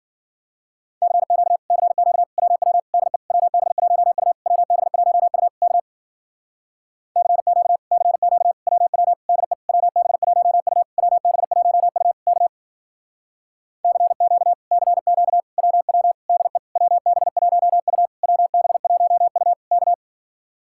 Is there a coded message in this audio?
Yes: Morse code from 1.0 to 19.9 s, heard as a keyed tone, about 53 wpm, 700 Hz. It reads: CQ CQ WW DE WB1U WB1U K CQ CQ WW DE WB1U WB1U K CQ CQ WW DE WB1U WB1U K